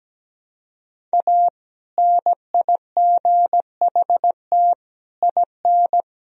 Morse code, heard as a keyed tone, about 17 wpm, 700 Hz. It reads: A NIGHT IN
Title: Sentences from Top 1000 Words - Review Part 1 - 17wpm